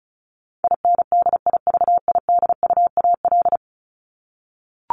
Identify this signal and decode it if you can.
Morse code, heard as a keyed tone, about 34 wpm, 700 Hz. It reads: INDIVIDUAL
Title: ES:S Suffix - Top 100 Words with Base Words 35wpm